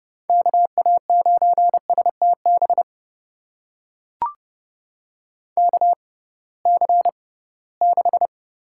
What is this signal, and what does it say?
Morse code, heard as a keyed tone, about 30 wpm, spaced as for 19 wpm, 700 Hz. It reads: KA9STB KC6
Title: USA Call Signs - 6 Characters - 6x Character Spacing - 30wpm